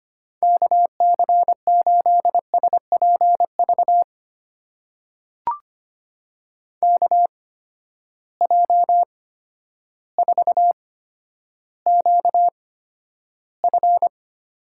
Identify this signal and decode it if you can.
Morse code, heard as a keyed tone, about 25 wpm, spaced as for 11 wpm, 700 Hz. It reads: KC8SPV KJ4QF
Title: USA Call Signs - 6 Characters - 8x Character Spacing - 25wpm